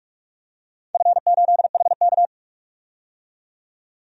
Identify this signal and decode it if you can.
Morse code, heard as a keyed tone, about 44 wpm, 700 Hz. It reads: F8HK
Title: DXCC-100 Call Signs - 4 Characters - 45wpm